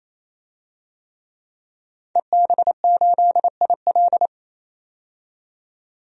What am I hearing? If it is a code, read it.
Morse code, heard as a keyed tone, about 28 wpm, 700 Hz. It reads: EB8IL
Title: DXCC-100 Call Signs - 5 Characters - 28wpm